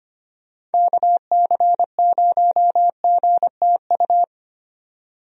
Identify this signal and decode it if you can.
Morse code, heard as a keyed tone, about 25 wpm, 700 Hz. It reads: KC0GTU
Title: USA Call Signs - 6 Characters - 25wpm